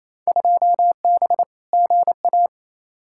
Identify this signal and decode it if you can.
Morse code, heard as a keyed tone, about 28 wpm, 700 Hz. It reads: 2B GA
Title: Field Day Exchanges Encore - 28wpm